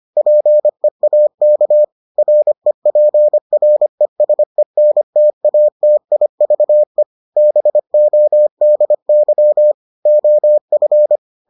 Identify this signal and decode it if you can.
Morse code, heard as a keyed tone, about 25 wpm, 600 Hz. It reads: PEAK REPRESENTATIVE BODY OF